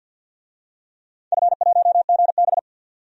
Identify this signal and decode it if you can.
Morse code, heard as a keyed tone, about 47 wpm, 700 Hz. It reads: F1GB
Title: DXCC-100 Call Signs - 4 Characters - 50wpm